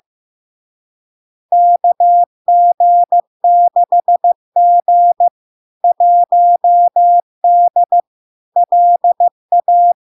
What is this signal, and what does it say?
Morse code, heard as a keyed tone, about 15 wpm, 700 Hz. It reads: KG6G 1D LA
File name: LARK-FD-2022-Log-in-CW-15wpm-700-Hz